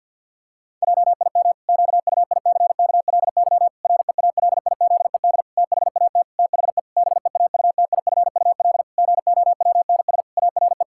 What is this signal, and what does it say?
Morse code, heard as a keyed tone, about 50 wpm, 700 Hz. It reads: JIM QUICKLY REALIZED THAT THE BEAUTIFUL GOWNS ARE
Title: Pangram Challenge v2 - 50wpm